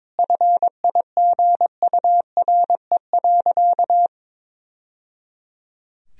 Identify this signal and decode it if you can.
Morse code, heard as a keyed tone, about 22 wpm, 700 Hz. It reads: FIGURE.
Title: Sentences from Top 1000 Words - Review Part 4 - 22wpm